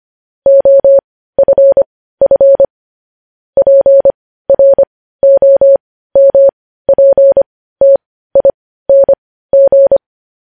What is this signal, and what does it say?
Morse code, heard as a keyed tone, about 25 wpm, spaced as for 15 wpm, 550 Hz. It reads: OFF PROMPTING
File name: podcast / News headlines-1x-25x15